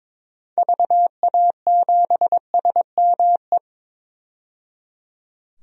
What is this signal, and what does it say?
Morse code, heard as a keyed tone, about 22 wpm, 700 Hz. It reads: VA7SME
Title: DXCC-100 Call Signs - 6 Characters - 22wpm